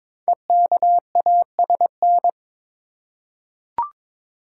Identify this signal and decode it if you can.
Morse code, heard as a keyed tone, about 22 wpm, 700 Hz. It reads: EKASN
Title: Code Groups - 5 Letters - 22wpm